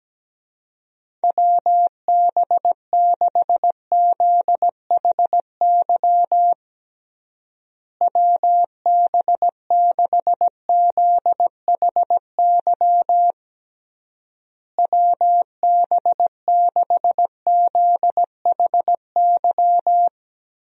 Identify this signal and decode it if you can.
Morse code, heard as a keyed tone, about 17 wpm, 700 Hz. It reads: WB6ZHY WB6ZHY WB6ZHY
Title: USA Call Signs - 6 Characters - Repeated 3x - 17wpm